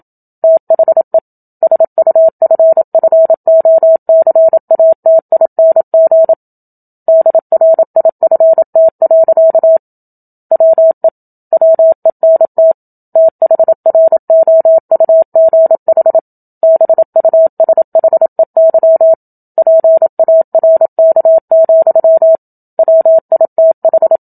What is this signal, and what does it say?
Morse code, heard as a keyed tone, about 27 wpm, 650 Hz. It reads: THE SUFFOCATING DRIFT. WE WENT THROUGH BUSHEY PARK, WITH
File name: Book Two / 01-Chapter 1 - 27 WPM